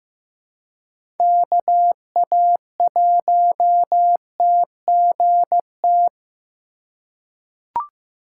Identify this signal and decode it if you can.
Morse code, heard as a keyed tone, about 15 wpm, 700 Hz. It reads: KA1TGT
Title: USA Call Signs - 6 Characters - 4x Character Spacing - 15wpm